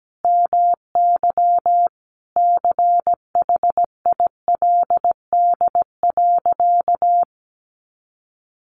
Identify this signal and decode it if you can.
Morse code, heard as a keyed tone, about 17 wpm, 700 Hz. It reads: MY CHILD.
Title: Sentences from Top 1300 Words 17wpm